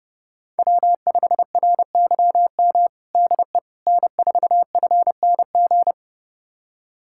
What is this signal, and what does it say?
Morse code, heard as a keyed tone, about 32 wpm, 700 Hz. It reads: W5RYM DE N4FNG
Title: USA Callsign DE Callsign - 5 Characters - 30wpm